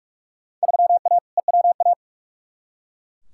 Morse code, heard as a keyed tone, about 42 wpm, 700 Hz. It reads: 3A EWA
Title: Field Day Exchanges Encore - 45wpm